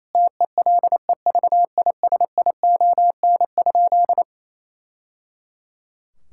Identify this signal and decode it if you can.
Morse code, heard as a keyed tone, about 27 wpm, 700 Hz. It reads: TELEVISION?
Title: Sentences from Top 1000 Words - Review Part 5 - 28wpm